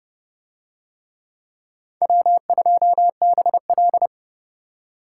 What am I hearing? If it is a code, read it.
Morse code, heard as a keyed tone, about 27 wpm, 700 Hz. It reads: W2BL